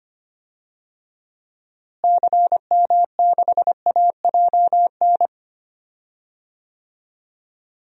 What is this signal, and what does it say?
Morse code, heard as a keyed tone, about 25 wpm, 700 Hz. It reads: CM6AJN
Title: DXCC-100 Call Signs - 6 Characters - Repeated 3x - 25wpm